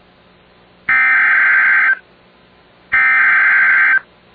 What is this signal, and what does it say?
Emergency Alert System (SAME) tones (header bursts), heard as harsh buzzing bursts. It is heard from 0.9 to 3.9 s.